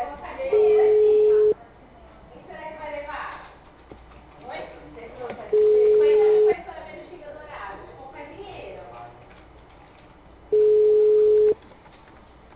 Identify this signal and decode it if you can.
Ringback tone (Europe, 425 Hz), with speech.
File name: out- -2019- - - .3790